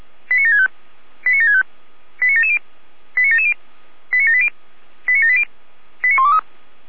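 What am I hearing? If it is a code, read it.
Five-tone selective call (ZVEI), heard as quick runs of five tones, tones of about 70 ms. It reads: 89765 89765 89770 89770 89780 89780 89123
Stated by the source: ZVEI five-tone selective call received on an SDR.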